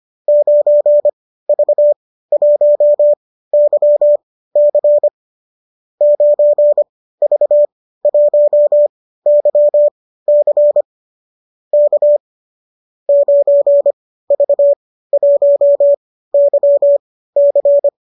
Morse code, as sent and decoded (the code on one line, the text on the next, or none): ----. ...- .---- -.-- -.-. / ----. ...- .---- -.-- -.-. / -.- / ----. ...- .---- -.-- -.-.
9V1YC 9V1YC K 9V1YC